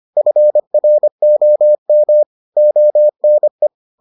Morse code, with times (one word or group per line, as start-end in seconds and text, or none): 0.2-2.2 FROM
2.6-3.7 ONE